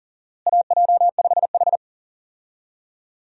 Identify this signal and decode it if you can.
Morse code, heard as a keyed tone, about 40 wpm, 700 Hz. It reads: AJ5H
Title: USA Call Signs - 4 Characters - Repeated 2x - 40wpm